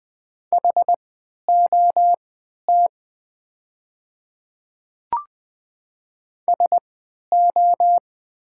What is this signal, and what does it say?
Morse code, heard as a keyed tone, about 20 wpm, spaced as for 11 wpm, 700 Hz. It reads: HOT SO
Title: 3x Character Spacing - Top 100 Words - 20wpm